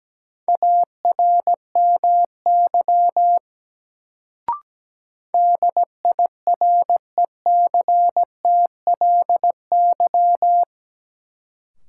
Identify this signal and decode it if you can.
Morse code, heard as a keyed tone, about 17 wpm, 700 Hz. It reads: ARMY DIRECTLY
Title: Sentences from Top 800 Words - New Word Drill - 17wpm